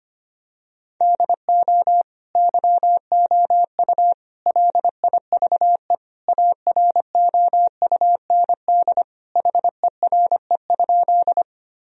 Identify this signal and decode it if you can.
Morse code, heard as a keyed tone, about 25 wpm, 700 Hz. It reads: DO YOU LIVE AROUND HERE?
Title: Sentences from Top 1000 Words - Review Part 3 - 25wpm